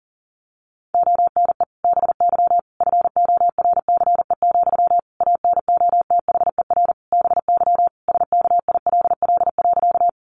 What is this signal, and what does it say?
Morse code, heard as a keyed tone, about 40 wpm, 700 Hz. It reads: ONE BY FORCE, ANOTHER BY SKILL.
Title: Sentences from Top 1200 Words 40wpm